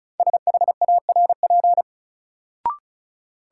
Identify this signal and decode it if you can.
Morse code, heard as a keyed tone, about 36 wpm, 700 Hz. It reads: SHARP
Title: Sentences from Top 1600 Words - New Word Drill Alt - 35wpm